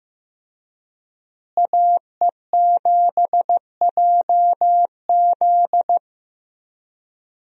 Morse code, heard as a keyed tone, about 15 wpm, 700 Hz. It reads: AE7JZ
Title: USA Call Signs - 5 Characters - Repeated 3x - 15wpm